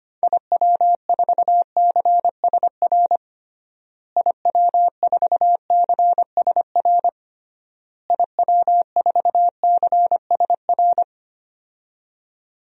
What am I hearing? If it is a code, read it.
Morse code, heard as a keyed tone, about 25 wpm, 700 Hz. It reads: IW4CSR IW4CSR IW4CSR
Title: DXCC-100 Call Signs - 6 Characters - Repeated 3x - 25wpm